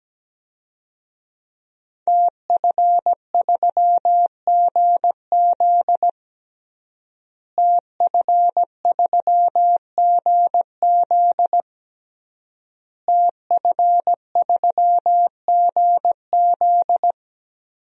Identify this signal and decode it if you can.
Morse code, heard as a keyed tone, about 17 wpm, 700 Hz. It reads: TF3GZ TF3GZ TF3GZ
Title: DXCC-100 Call Signs - 5 Characters - Repeated 3x - 17wpm